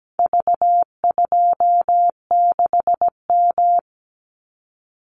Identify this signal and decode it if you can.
Morse code, heard as a keyed tone, about 17 wpm, 700 Hz. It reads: V26M
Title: Call Signs Repeated 2x - 17wpm